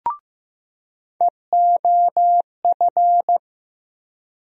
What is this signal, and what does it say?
Morse code, heard as a keyed tone, about 15 wpm, 700 Hz. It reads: EOF
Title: USA Call Sign Postfixes - 3 Letters 15wpm